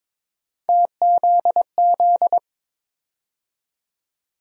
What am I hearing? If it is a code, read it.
Morse code, heard as a keyed tone, about 22 wpm, 700 Hz. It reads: TZZ
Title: USA Call Sign Postfixes - 3 Letters 22wpm